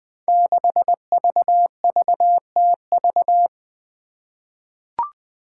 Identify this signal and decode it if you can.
Morse code, heard as a keyed tone, about 20 wpm, 700 Hz. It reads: 6VVTV
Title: Five Challenging Letter-Number Combinations 20wpm